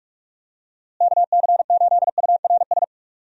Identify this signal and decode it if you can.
Morse code, heard as a keyed tone, about 46 wpm, 700 Hz. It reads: KC8URS
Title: USA Call Signs - 6 Characters - Repeated 3x - 45wpm